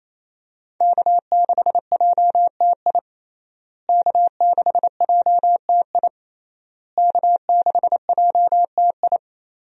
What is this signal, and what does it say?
Morse code, heard as a keyed tone, about 28 wpm, 700 Hz. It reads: K6JTI K6JTI K6JTI
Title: USA Call Signs - 5 Characters - Repeated 3x - 28wpm